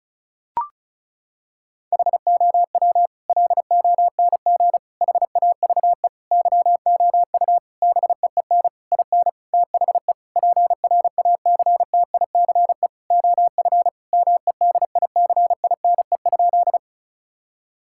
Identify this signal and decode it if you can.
Morse code, heard as a keyed tone, about 36 wpm, 700 Hz. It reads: HOW LONG HAVE YOU BEEN IN THE PRACTICE OF MEDICINE?
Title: Sentences from Top 1000 Words - Review Part 6 - 35wpm